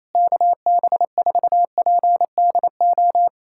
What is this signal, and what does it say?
Morse code, heard as a keyed tone, about 28 wpm, 700 Hz. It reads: KB4PDO
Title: USA Callsign DE Callsign - 6 Characters - 28wpm